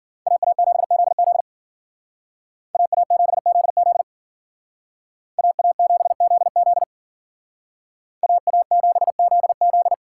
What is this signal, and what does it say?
Morse code, heard as a keyed tone, about 52 wpm, 700 Hz. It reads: AA7ZZ AA7ZZ AA7ZZ AA7ZZ